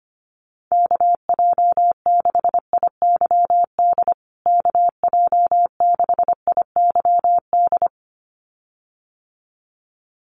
Morse code, heard as a keyed tone, about 25 wpm, 700 Hz. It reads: KJ6IYD KJ6IYD
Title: USA Call Signs Repeated 2x 25wpm